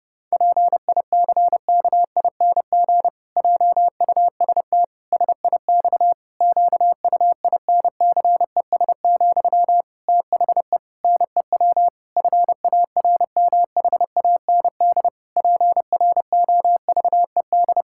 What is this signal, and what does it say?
Morse code, heard as a keyed tone, about 30 wpm, 700 Hz. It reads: PICKING JUST SIX QUINCES, THE NEW FARMHAND PROVED
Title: Pangram Challenge v2 - 30wpm